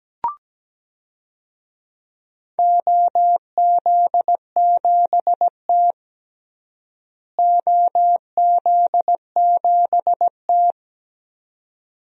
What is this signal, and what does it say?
Morse code, heard as a keyed tone, about 17 wpm, 700 Hz. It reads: OZ7T OZ7T
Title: DXCC-100 Call Signs - 4 Characters - Repeated 2x - 17wpm